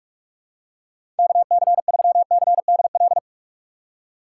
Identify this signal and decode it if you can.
Morse code, heard as a keyed tone, about 44 wpm, 700 Hz. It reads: KC3CDL
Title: USA Call Signs - 6 Characters - 3x Character Spacing - 45wpm